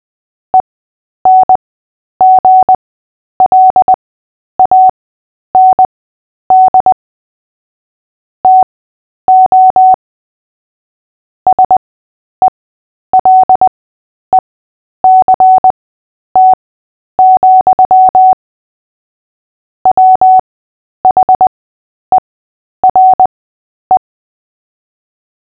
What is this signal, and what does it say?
Morse code, heard as a keyed tone, about 20 wpm, spaced as for 10 wpm, 750 Hz. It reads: ENGLAND TO SELECT, WHERE
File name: Short-Story-7-10wpm